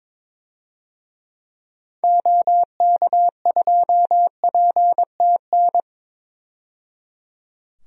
Morse code, heard as a keyed tone, about 21 wpm, 700 Hz. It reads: OK2PTN
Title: DXCC-100 Call Signs - 6 Characters - 22wpm